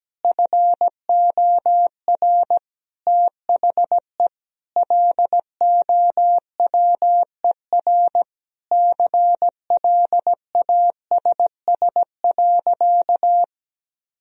Morse code, heard as a keyed tone, about 17 wpm, 700 Hz. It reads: FOR THE LOWER CLASS.
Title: Sentences from Top 1000 Words - Review Part 4 - 17wpm